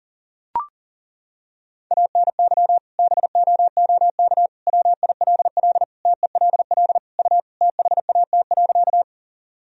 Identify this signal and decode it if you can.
Morse code, heard as a keyed tone, about 40 wpm, 700 Hz. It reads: ANY BOOK WILL TELL U THAT.
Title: Sentences from Top 300 Words and Abbreviations 40wpm